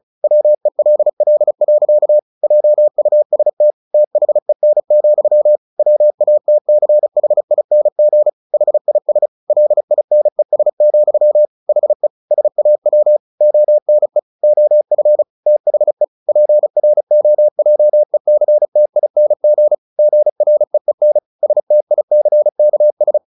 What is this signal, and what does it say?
Morse code, heard as a keyed tone, about 34 wpm, 600 Hz. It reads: WELL. JUST THEN, WATCHING HIS LINES, HE SAW ONE OF THE PROJECTING GREEN STICKS
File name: the old man and the sea / the old man and the sea 1 35wpm